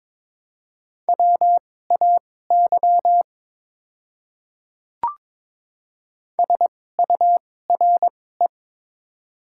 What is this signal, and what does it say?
Morse code, heard as a keyed tone, about 21 wpm, spaced as for 16 wpm, 700 Hz. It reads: WAY SURE
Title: 2x Character Spacing - Top 300 Words - 22wpm